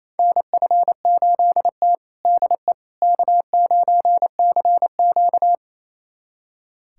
Morse code, heard as a keyed tone, about 28 wpm, 700 Hz. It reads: NF8T DE K9CQ
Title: USA Callsign DE Callsign - 4 Characters - 28wpm